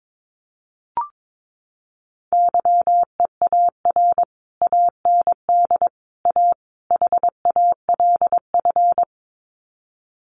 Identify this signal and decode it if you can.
Morse code, heard as a keyed tone, about 22 wpm, 700 Hz. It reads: YEAR AND A HALF
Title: Sets of 4 Words 22wpm